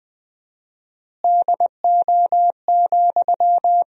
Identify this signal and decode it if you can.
Morse code, heard as a keyed tone, about 20 wpm, 700 Hz. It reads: DO,
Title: 4x Word-Spacing - Sentences from Top 100 Words - 20wpm